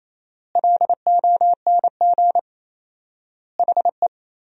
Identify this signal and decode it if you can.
Morse code, heard as a keyed tone, about 28 wpm, 700 Hz. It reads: LONG HE
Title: 4x Word-Spacing - Sentences from Top 300 Words - 28wpm